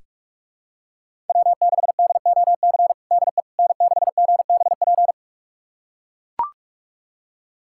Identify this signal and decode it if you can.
Morse code, heard as a keyed tone, about 41 wpm, 700 Hz. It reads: W6DOC DE N6GBP
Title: USA Callsign DE Callsign - 5 Characters - 45wpm